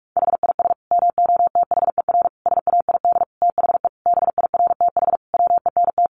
Morse code, heard as a keyed tone, about 44 wpm, 700 Hz. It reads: HIS MOTHER SAID THE BIRTH WENT
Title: Sentences from Top 800 Words 45wpm